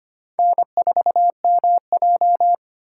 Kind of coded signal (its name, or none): Morse code